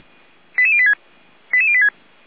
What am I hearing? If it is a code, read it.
Five-tone selective call (ZVEI), heard as quick runs of five tones, tones of about 70 ms. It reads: 88086 88086
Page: SDR/ZVEI